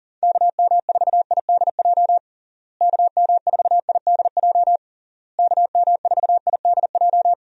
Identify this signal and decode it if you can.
Morse code, heard as a keyed tone, about 40 wpm, 700 Hz. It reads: KM4IDJ KM4IDJ KM4IDJ
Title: USA Call Signs - 6 Characters - Repeated 3x - 40wpm